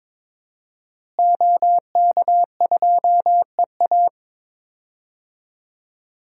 Morse code, heard as a keyed tone, about 22 wpm, 700 Hz. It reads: OK2EA
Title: DXCC-100 Call Signs - 5 Characters - Repeated 2x - 22wpm